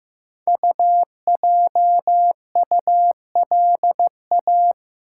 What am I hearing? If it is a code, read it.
Morse code, heard as a keyed tone, about 15 wpm, 700 Hz. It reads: UJULA
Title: Code Groups - 5 Letters - 15wpm